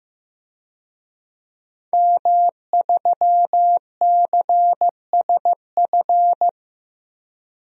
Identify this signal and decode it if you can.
Morse code, heard as a keyed tone, about 15 wpm, 700 Hz. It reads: M3CSF
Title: DXCC-100 Call Signs - 5 Characters - 15wpm